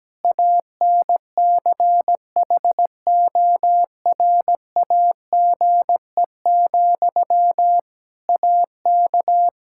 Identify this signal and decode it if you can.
Morse code, heard as a keyed tone, about 17 wpm, 700 Hz. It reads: ANCHORAGE, AK
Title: Practice QSO - Exchange 1 - Beginning - 17wpm